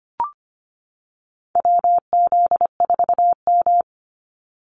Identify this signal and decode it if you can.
Morse code, heard as a keyed tone, about 25 wpm, 690 Hz. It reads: WZ4M
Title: NA SOTA Activator Callsigns 25wpm